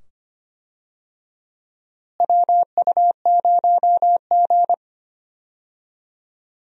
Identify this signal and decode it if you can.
Morse code, heard as a keyed tone, about 25 wpm, 700 Hz. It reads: WU0G
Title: USA Call Signs - 4 Characters - Repeated 2x - 25wpm